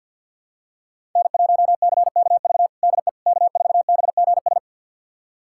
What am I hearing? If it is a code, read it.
Morse code, heard as a keyed tone, about 50 wpm, 700 Hz. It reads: N1CKV DE K4BGS
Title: USA Callsign DE Callsign - 5 Characters - 50wpm